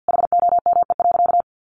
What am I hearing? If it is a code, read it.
Morse code, heard as a keyed tone, about 50 wpm, 700 Hz. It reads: HOME.
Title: Sentences from Top 1400 Words 50wpm